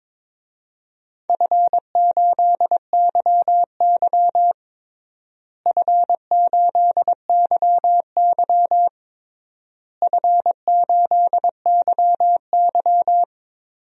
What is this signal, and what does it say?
Morse code, heard as a keyed tone, about 21 wpm, 700 Hz. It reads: F8YY F8YY F8YY
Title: DXCC-100 Call Signs - 4 Characters - Repeated 3x - 22wpm